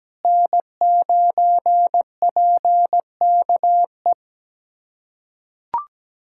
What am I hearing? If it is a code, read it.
Morse code, heard as a keyed tone, about 17 wpm, 700 Hz. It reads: N9PKE
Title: USA Callsign DE Callsign - 5 Characters - 17wpm